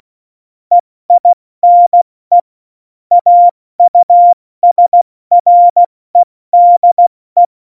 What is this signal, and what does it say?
Morse code, heard as a keyed tone, about 16 wpm, spaced as for 14 wpm, 700 Hz. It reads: EINE AUSREDE
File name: Morsekurs / Witze / Witz 30. April 2019, 70 F80